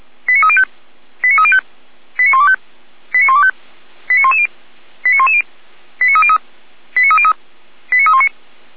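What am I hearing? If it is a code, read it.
Five-tone selective call (ZVEI), heard as quick runs of five tones, tones of about 70 ms. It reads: 89305 89305 89125 89125 89110 89110 89303 89303 89319
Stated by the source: ZVEI five-tone selective call received by SDR, from the audio server of Feuerwehr Orlamünde (fire brigade).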